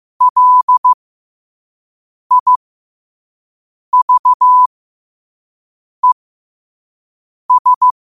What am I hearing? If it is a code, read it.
Morse code, heard as a keyed tone, about 15 wpm, spaced as for 5 wpm, 1000 Hz. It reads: LIVES
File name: qotdfiles / 5wpm